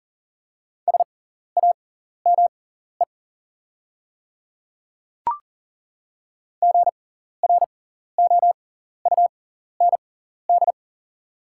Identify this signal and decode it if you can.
Morse code, heard as a keyed tone, about 37 wpm, spaced as for 14 wpm, 700 Hz. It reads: SAME GROUND
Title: 6x Character Spacing - Top 500 Words - 40wpm